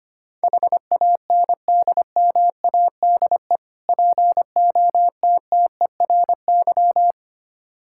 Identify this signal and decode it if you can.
Morse code, heard as a keyed tone, about 25 wpm, 700 Hz. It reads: HANDMADE POTTERY
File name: practice / 25wpm / Difficult Words with Hints - 25wpm